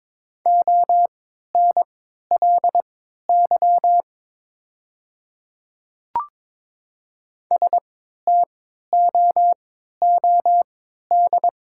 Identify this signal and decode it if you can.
Morse code, heard as a keyed tone, about 22 wpm, spaced as for 12 wpm, 700 Hz. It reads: ONLY STOOD
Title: 3x Character Spacing - Top 500 Words - 22wpm